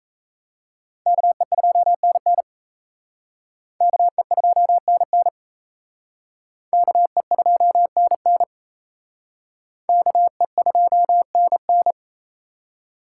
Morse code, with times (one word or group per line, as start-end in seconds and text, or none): 1.1-2.4 KE2NN
3.8-5.3 KE2NN
6.7-8.4 KE2NN
9.9-11.9 KE2NN